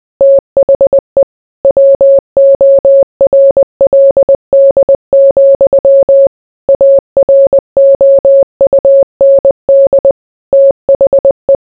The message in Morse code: THE WORLD, AROUND THE

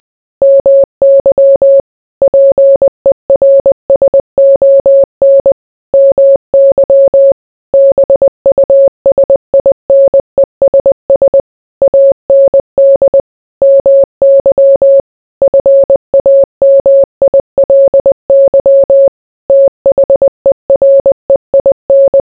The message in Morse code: MY PERSON MY BUSINESS AND MY FAMILY THEREIN

